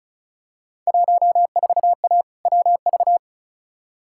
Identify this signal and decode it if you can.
Morse code, heard as a keyed tone, about 34 wpm, 700 Hz. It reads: 14A WV
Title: Field Day Exchanges Encore - 35wpm